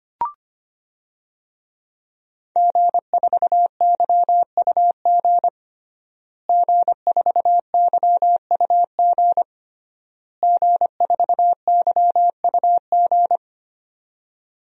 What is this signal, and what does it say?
Morse code, heard as a keyed tone, about 25 wpm, 700 Hz. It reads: G4YUG G4YUG G4YUG
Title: DXCC-100 Call Signs - 5 Characters - Repeated 3x - 25wpm